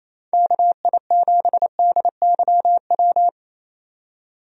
Morse code, heard as a keyed tone, about 28 wpm, 700 Hz. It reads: KI7DYW
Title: USA Call Signs - 6 Characters - 28wpm